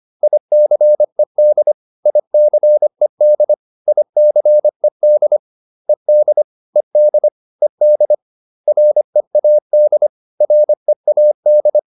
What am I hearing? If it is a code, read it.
Morse code, heard as a keyed tone, about 25 wpm, 600 Hz. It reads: ICED ICED ICED ED ED ED READ READ